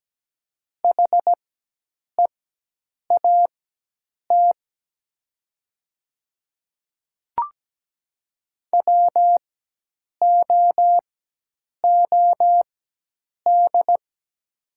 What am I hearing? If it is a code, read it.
Morse code, heard as a keyed tone, about 17 wpm, spaced as for 8 wpm, 700 Hz. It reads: HEAT WOOD
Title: 4x Character Spacing - Top 500 Words - 17wpm